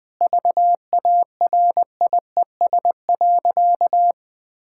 Morse code, heard as a keyed tone, about 20 wpm, 700 Hz. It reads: VARIES.
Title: Sentences with 7-Word Phrases - Spoken Beginning - 20wpm